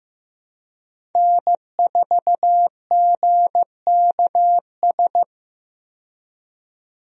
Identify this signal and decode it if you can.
Morse code, heard as a keyed tone, about 15 wpm, 700 Hz. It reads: N4GKS
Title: USA Call Signs - 5 Characters - 15wpm